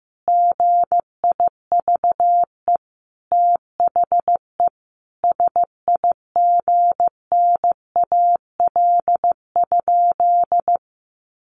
Morse code, heard as a keyed tone, about 15 wpm, 700 Hz. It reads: GIVE THE SIGNAL?
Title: Sentences from Top 1400 Words 15wpm